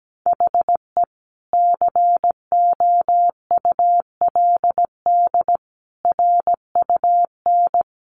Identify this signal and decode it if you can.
Morse code, heard as a keyed tone, about 17 wpm, 700 Hz. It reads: HE COULD RUN
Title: Sentences from Top 1100 Words 17wpm